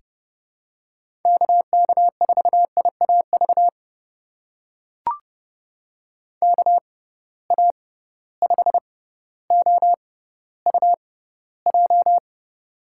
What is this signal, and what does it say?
Morse code, heard as a keyed tone, about 31 wpm, spaced as for 16 wpm, 700 Hz. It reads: KK4IAV KA5OUJ
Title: USA Call Signs - 6 Characters - 6x Character Spacing - 30wpm